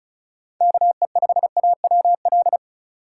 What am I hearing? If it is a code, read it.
Morse code, heard as a keyed tone, about 36 wpm, 700 Hz. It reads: KE5AWL
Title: USA Call Signs - 4x Character Spacing - 35wpm